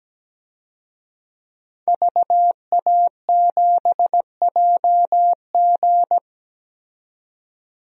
Morse code, heard as a keyed tone, about 17 wpm, 700 Hz. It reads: VA7JG
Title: DXCC-100 Call Signs - 5 Characters - 17wpm